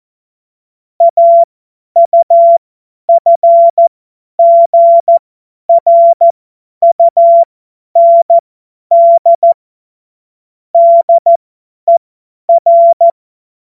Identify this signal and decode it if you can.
Morse code, heard as a keyed tone, about 14 wpm, spaced as for 10 wpm, 680 Hz. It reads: AUFGRUND DER
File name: Morsekurs / Witze / Witz 11. Januar 2022, 50 F70